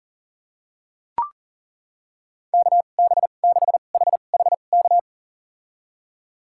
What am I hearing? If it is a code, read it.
Morse code, heard as a keyed tone, about 37 wpm, spaced as for 28 wpm, 700 Hz. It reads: KB6HHK
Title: USA Call Signs - 2x Character Spacing - 40wpm